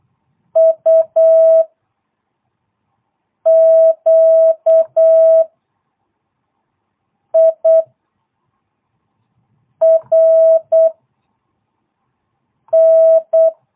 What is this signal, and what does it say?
Morse code, heard as a keyed tone, about 8 wpm, 660 Hz. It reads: U Q I R N